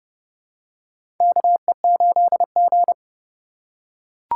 Morse code, heard as a keyed tone, about 31 wpm, 700 Hz. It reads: KE8G
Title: USA Call Signs - 4x Character Spacing - 30wpm